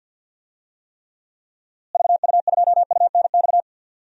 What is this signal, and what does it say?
Morse code, heard as a keyed tone, about 47 wpm, 700 Hz. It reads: VU2UNX
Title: DXCC-100 Call Signs - 6 Characters - Repeated 3x - 50wpm